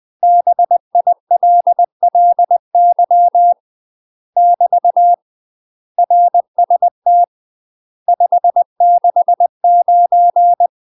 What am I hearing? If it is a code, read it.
Morse code, heard as a keyed tone, about 20 wpm, 700 Hz. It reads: BILLY <BT> RST 569
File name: UK MASTER QSOs TRAINING / 20WPM / UK MASTER QSOs 0073